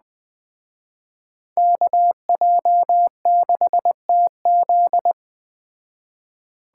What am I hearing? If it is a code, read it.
Morse code, heard as a keyed tone, about 20 wpm, 700 Hz. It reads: KJ6TZ
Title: Speed Racing - USA 2x2 Call Signs 17wpm